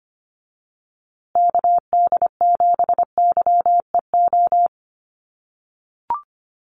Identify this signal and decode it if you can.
Morse code, heard as a keyed tone, about 25 wpm, 700 Hz. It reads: KD7YEO
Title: USA Call Signs Repeated 2x 25wpm